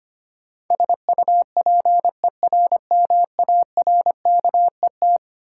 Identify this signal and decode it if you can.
Morse code, heard as a keyed tone, about 25 wpm, 700 Hz. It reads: SUPERMARKET
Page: morse-code-ninja-practice-25wpm